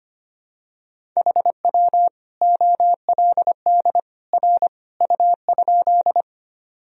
Morse code, heard as a keyed tone, about 25 wpm, 700 Hz. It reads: HW OLD R U?
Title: Sentences from Top 200 Words and Abbreviations 25wpm